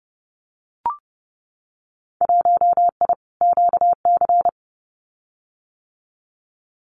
Morse code, heard as a keyed tone, about 30 wpm, 700 Hz. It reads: 1I QC
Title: Winter Field Day Exchanges 30wpm